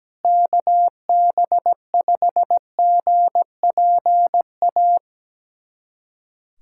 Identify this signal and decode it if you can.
Morse code, heard as a keyed tone, about 17 wpm, 700 Hz. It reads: KB5GPA
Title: USA Callsign DE Callsign - 6 Characters - 17wpm